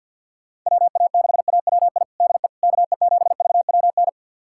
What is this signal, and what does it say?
Morse code, heard as a keyed tone, about 52 wpm, 700 Hz. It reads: WA6AWI DE KE7VWN
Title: USA Callsign DE Callsign - 6 Characters - 50wpm